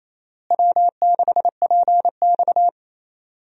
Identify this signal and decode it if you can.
Morse code, heard as a keyed tone, about 28 wpm, 700 Hz. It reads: W6PX
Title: USA Call Signs - 4 Characters - Repeated 3x - 28wpm